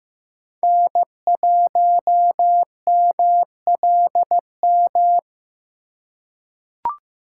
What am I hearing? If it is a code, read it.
Morse code, heard as a keyed tone, about 15 wpm, 700 Hz. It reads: N1MLM
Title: USA Call Signs - 4x Character Spacing - 15wpm